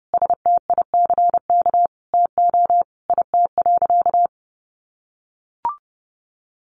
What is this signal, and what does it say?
Morse code, heard as a keyed tone, about 30 wpm, 700 Hz. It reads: STICK TO IT.
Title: Sentences from Top 1200 Words 30wpm